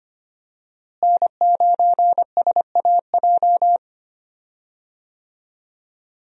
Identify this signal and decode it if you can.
Morse code, heard as a keyed tone, about 25 wpm, 700 Hz. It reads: N9SAJ
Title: USA Call Signs - 5 Characters - Repeated 3x - 25wpm